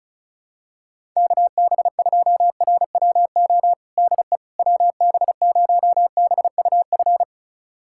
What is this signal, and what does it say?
Morse code, heard as a keyed tone, about 34 wpm, 700 Hz. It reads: KB2RWO DE WB0BUF
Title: USA Callsign DE Callsign - 6 Characters - 35wpm